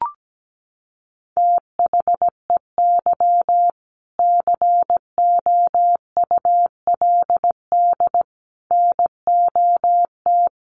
Morse code, heard as a keyed tone, about 17 wpm, 700 Hz. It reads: THEY COULD NOT
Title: Sentences from Top 1100 Words 17wpm